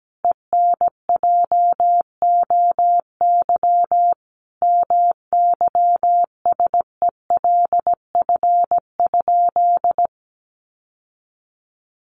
Morse code, heard as a keyed tone, about 17 wpm, 700 Hz. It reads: ENJOY MYSELF?
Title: Sentences from Top 1300 Words 17wpm